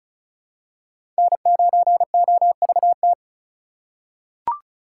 Morse code, heard as a keyed tone, about 36 wpm, 700 Hz. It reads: N9OVT
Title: USA Call Signs - 4x Character Spacing - 35wpm